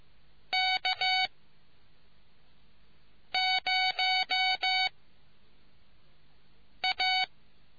Morse code, heard as a keyed tone, about 15 wpm, 750 Hz. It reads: K 0 A